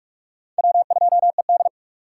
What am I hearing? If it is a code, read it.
Morse code, heard as a keyed tone, about 59 wpm, 700 Hz. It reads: W2ED